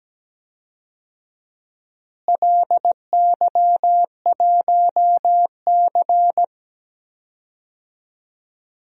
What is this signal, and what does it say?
Morse code, heard as a keyed tone, about 17 wpm, 700 Hz. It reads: LY1C
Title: DXCC-100 Call Signs - 4 Characters - Repeated 3x - 17wpm